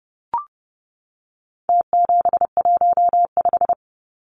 Morse code, heard as a keyed tone, about 30 wpm, 700 Hz. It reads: T715
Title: Numbers - Contest Style - 4-Digit 30wpm